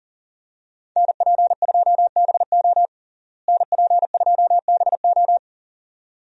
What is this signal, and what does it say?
Morse code, heard as a keyed tone, about 40 wpm, 700 Hz. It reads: NP2BO NP2BO
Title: USA Call Signs - 5 Characters - Repeated 2x - 40wpm